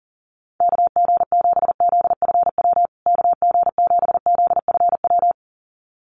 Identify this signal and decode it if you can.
Morse code, heard as a keyed tone, about 40 wpm, 700 Hz. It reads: KG7ZFW KG7ZFW
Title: USA Call Signs Repeated 2x 40wpm